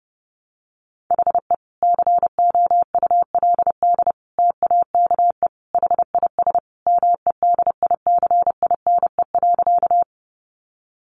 Morse code, heard as a keyed tone, about 30 wpm, 700 Hz. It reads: HE COULD TAKE HIS MEDICINE.
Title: Sentences from Top 900 Words 30wpm